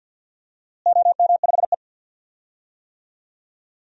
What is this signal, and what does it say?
Morse code, heard as a keyed tone, about 49 wpm, 700 Hz. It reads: OM5E
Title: DXCC-100 Call Signs - 4 Characters - Repeated 2x - 50wpm